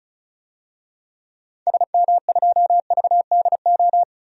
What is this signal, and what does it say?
Morse code, heard as a keyed tone, about 34 wpm, 700 Hz. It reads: SM2VDO
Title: DXCC-100 Call Signs - 6 Characters - 35wpm